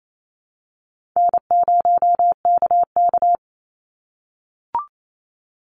Morse code, heard as a keyed tone, about 28 wpm, 700 Hz. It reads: N0KK
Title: Easy Call Signs - 28wpm